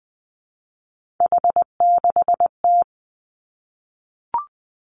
Morse code, heard as a keyed tone, about 20 wpm, 700 Hz. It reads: H6T
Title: Three Challenging Letter-Number Combinations 20wpm